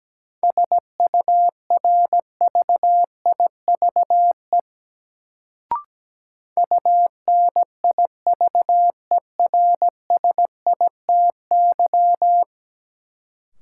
Morse code, heard as a keyed tone, about 17 wpm, 700 Hz. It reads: SURVIVE UNIVERSITY
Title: Sentences from Top 1300 Words - New Word Drill Alt - 17wpm